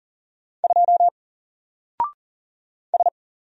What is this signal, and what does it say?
Morse code, heard as a keyed tone, about 38 wpm, 700 Hz. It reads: 2 S
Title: Single Character (Letter:Number:Punctuation:Prosign) Alt 40wpm